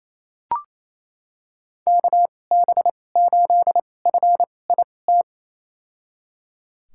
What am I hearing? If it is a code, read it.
Morse code, heard as a keyed tone, about 28 wpm, spaced as for 20 wpm, 700 Hz. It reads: KB8FIT
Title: USA Call Signs - 2x Character Spacing - 28wpm